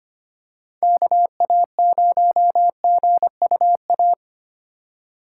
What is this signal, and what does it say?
Morse code, heard as a keyed tone, about 25 wpm, 700 Hz. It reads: KA0GUA
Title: USA Call Signs - 6 Characters - 4x Character Spacing - 25wpm